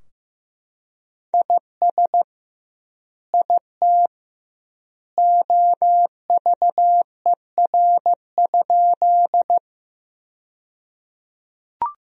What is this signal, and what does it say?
Morse code, heard as a keyed tone, about 15 wpm, 700 Hz. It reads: IS IT OVER?
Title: 2x Word-Spacing - Sentences from Top 100 Words - 15wpm